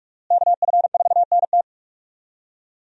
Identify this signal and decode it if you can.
Morse code, heard as a keyed tone, about 46 wpm, 700 Hz. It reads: KF4NT
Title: USA Call Signs - 5 Characters - Repeated 3x - 45wpm